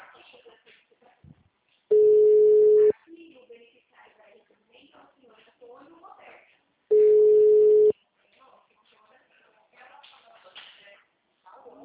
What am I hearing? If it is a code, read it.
Ringback tone (Europe, 425 Hz), with speech.
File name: out- -2011- - - .19996